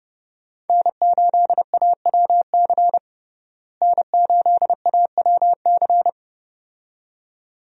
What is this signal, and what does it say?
Morse code, heard as a keyed tone, about 30 wpm, 700 Hz. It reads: N8AWC N8AWC
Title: USA Call Signs - 5 Characters - Repeated 2x - 30wpm